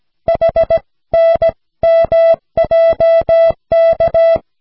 Morse code, heard as a keyed tone, about 16 wpm, 660 Hz. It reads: HNMJK
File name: cw / snelheid 4